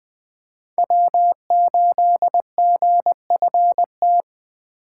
Morse code, heard as a keyed tone, about 20 wpm, 700 Hz. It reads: W8GFT
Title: Speed Racing - USA 1x3 Call Signs 17wpm